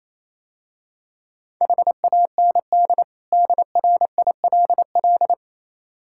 Morse code, heard as a keyed tone, about 27 wpm, 700 Hz. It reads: HAND DRILL
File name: Easy Words - Speed-Racing with Hints - 22wpm